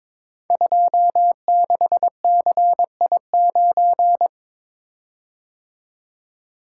Morse code, heard as a keyed tone, about 22 wpm, 700 Hz. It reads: 26CI9
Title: Code Groups - 5 Letter:Numbers - 22wpm